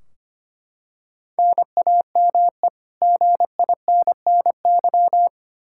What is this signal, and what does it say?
Morse code, heard as a keyed tone, about 25 wpm, 700 Hz. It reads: NAME GINNY
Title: Practice QSO - Exchange 1 - Beginning - 25wpm